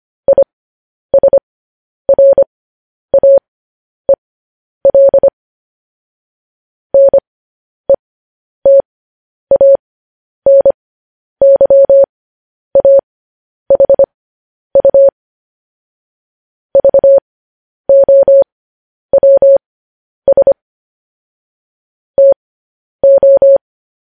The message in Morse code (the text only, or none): ISRAEL NETANYAHU VOWS TO